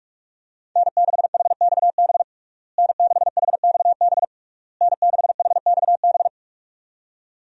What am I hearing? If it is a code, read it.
Morse code, heard as a keyed tone, about 41 wpm, 700 Hz. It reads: N6HXB N6HXB N6HXB
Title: USA Call Signs - 5 Characters - Repeated 3x - 45wpm